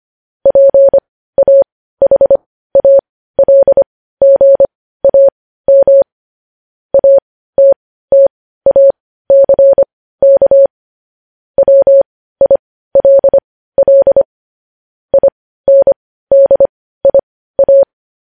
Morse code, heard as a keyed tone, about 25 wpm, spaced as for 15 wpm, 550 Hz. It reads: PAHALGAM ATTACK WILL INDIA
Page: news-headlines-in-morse-code-at-15-wpm